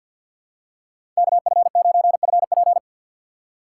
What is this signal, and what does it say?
Morse code, heard as a keyed tone, about 52 wpm, 700 Hz. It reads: KF9FP
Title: USA Call Signs - 5 Characters - 6x Character Spacing - 50wpm